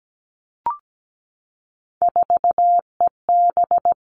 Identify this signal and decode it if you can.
Morse code, heard as a keyed tone, about 17 wpm, 700 Hz. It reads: 4EB
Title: All Cut Numbers - 3-Digit 17wpm